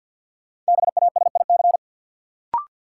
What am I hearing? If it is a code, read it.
Morse code, heard as a keyed tone, about 53 wpm, 700 Hz. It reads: BASIC
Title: Sentences from Top 1000 Words - Review Part 3 - New Word Drill - 50wpm